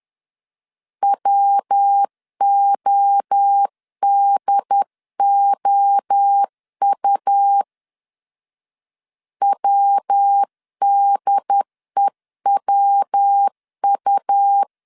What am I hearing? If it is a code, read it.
Morse code, heard as a keyed tone, about 10 wpm, 800 Hz. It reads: WODOU WDEWU
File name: 10wpm